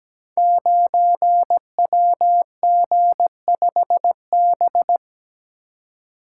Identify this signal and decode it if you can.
Morse code, heard as a keyed tone, about 17 wpm, 700 Hz. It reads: 9WG5B
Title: Code Groups - 5 Letter:Numbers - 17wpm